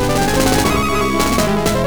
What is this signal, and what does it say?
There is no code.